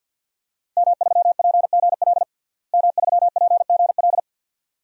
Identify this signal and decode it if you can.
Morse code, heard as a keyed tone, about 51 wpm, 700 Hz. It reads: M3PGL M3PGL
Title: DXCC-100 Call Signs - 5 Characters - Repeated 2x - 50wpm